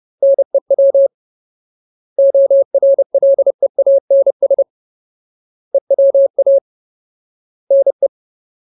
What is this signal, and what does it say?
Morse code, heard as a keyed tone, about 30 wpm, 550 Hz. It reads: NEW ORLEANS EWA NE
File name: Ultimate Word List / 30WPM Ultimate